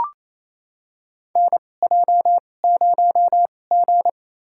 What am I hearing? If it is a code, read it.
Morse code, heard as a keyed tone, about 28 wpm, spaced as for 20 wpm, 700 Hz. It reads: NJ0G
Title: USA Call Signs - 4 Characters - 2x Character Spacing - 28wpm